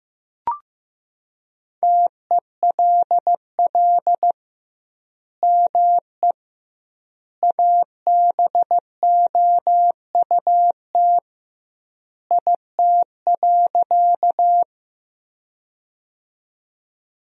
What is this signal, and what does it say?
Morse code, heard as a keyed tone, about 15 wpm, 700 Hz. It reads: TELL ME ABOUT IT.
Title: 2x Word-Spacing - Sentences from Top 200 Words - 15wpm